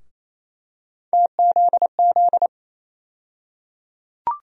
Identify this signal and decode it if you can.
Morse code, heard as a keyed tone, about 28 wpm, 700 Hz. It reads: TZZ
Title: USA Call Sign Postfixes - 3 Letters 28wpm